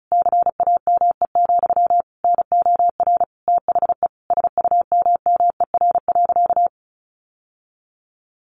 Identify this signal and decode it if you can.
Morse code, heard as a keyed tone, about 34 wpm, 700 Hz. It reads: CAME, NOR THE SUMMER.
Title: Sentences from Top 1500 Words 35wpm